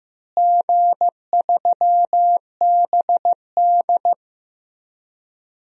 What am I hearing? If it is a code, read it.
Morse code, heard as a keyed tone, about 15 wpm, 700 Hz. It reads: G3BD
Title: DXCC-100 Call Signs - 4 Characters - Repeated 2x - 15wpm